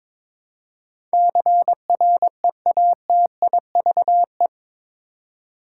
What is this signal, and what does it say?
Morse code, heard as a keyed tone, about 22 wpm, 700 Hz. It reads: CREATIVE